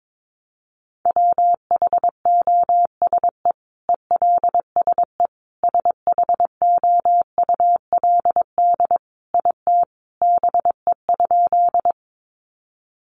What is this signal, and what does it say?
Morse code, heard as a keyed tone, about 22 wpm, 700 Hz. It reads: WHOSE ELSE SHOULD IT BE?
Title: Sentences from Top 1200 Words 22wpm